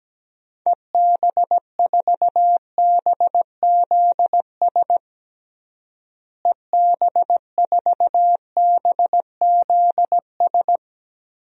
Morse code, heard as a keyed tone, about 17 wpm, 700 Hz. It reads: EB4BZS EB4BZS
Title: DXCC-100 Call Signs - 6 Characters - Repeated 3x - 17wpm